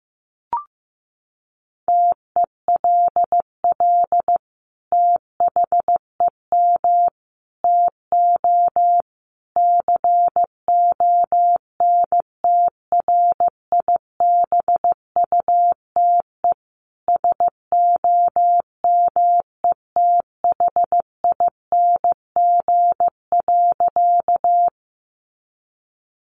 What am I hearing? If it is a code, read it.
Morse code, heard as a keyed tone, about 15 wpm, 700 Hz. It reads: TELL THEM TO CONTRIBUTE SOMETHING.
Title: Sentences from Top 1400 Words 15wpm